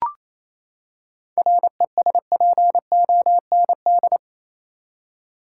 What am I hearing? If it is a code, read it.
Morse code, heard as a keyed tone, about 28 wpm, 700 Hz. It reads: RESPOND
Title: Seven Letter Words - 28wpm